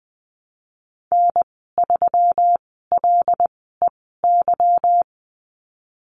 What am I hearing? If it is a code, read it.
Morse code, heard as a keyed tone, about 20 wpm, spaced as for 15 wpm, 700 Hz. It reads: N3LEY
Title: 147 Lesson 24g - Letter y - Call Signs 20wpm